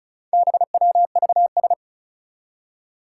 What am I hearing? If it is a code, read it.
Morse code, heard as a keyed tone, about 36 wpm, 700 Hz. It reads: BWVS
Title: Four Challenging Letter-Number Combinations 35wpm